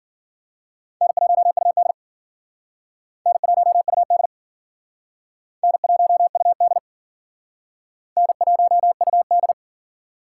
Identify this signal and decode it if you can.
Morse code, heard as a keyed tone, about 50 wpm, 700 Hz. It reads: N1UD N1UD N1UD N1UD
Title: Speed Racing - USA 1x2 Call Signs 40wpm